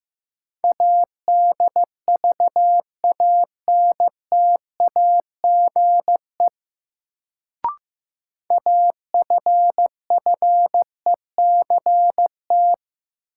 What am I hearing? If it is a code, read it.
Morse code, heard as a keyed tone, about 15 wpm, 700 Hz. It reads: ADVANTAGE AFFECT
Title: Sentences from Top 900 Words - New Word Drill - 15wpm